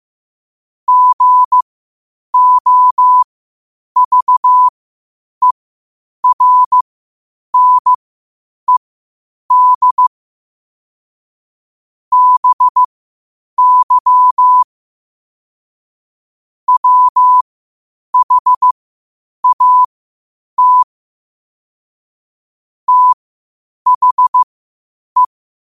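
Morse code, heard as a keyed tone, about 15 wpm, spaced as for 8 wpm, 1000 Hz. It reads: GOVERNED BY WHAT THE
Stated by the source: Morse code at 8 wpm.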